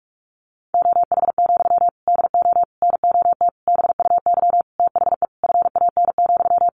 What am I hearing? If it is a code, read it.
Morse code, heard as a keyed tone, about 44 wpm, 700 Hz. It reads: OH, DO NOT BUY THE FAN,
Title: Sentences from Top 1300 Words 45wpm